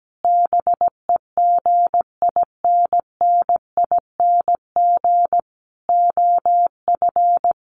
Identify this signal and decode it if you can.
Morse code, heard as a keyed tone, about 17 wpm, 700 Hz. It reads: BEGINNING OF
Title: Sentences from Top 900 Words 17wpm